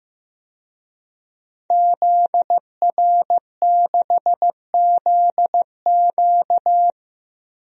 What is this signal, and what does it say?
Morse code, heard as a keyed tone, about 15 wpm, 700 Hz. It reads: ZR6ZQ
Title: DXCC-100 Call Signs - 5 Characters - 15wpm